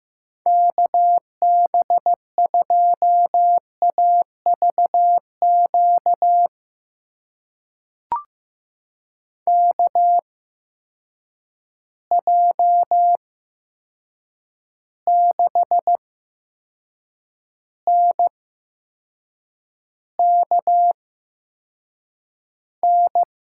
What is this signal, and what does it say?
Morse code, heard as a keyed tone, about 15 wpm, spaced as for 6 wpm, 700 Hz. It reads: KB2AVQ KJ6NKN
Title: USA Call Signs - 6 Characters - 8x Character Spacing - 15wpm